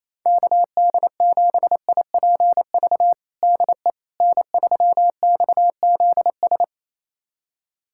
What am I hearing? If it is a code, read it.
Morse code, heard as a keyed tone, about 28 wpm, 700 Hz. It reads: KD7IPV DE N3XZS
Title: USA Callsign DE Callsign 28wpm